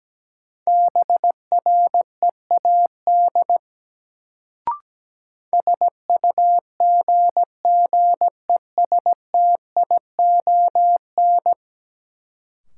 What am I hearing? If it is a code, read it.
Morse code, heard as a keyed tone, about 17 wpm, 700 Hz. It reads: BREAD SUGGESTION
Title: Sentences from Top 1600 Words - New Word Drill - 17wpm